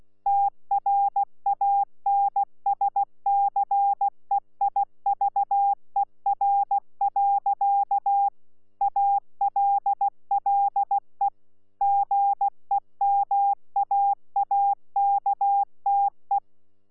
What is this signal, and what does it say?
Morse code, heard as a keyed tone, about 16 wpm, 800 Hz. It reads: TRANSCEIVER. ALLE GEMAAKTE
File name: Z PI4VRZ / PI4VRZ 14-jun-2025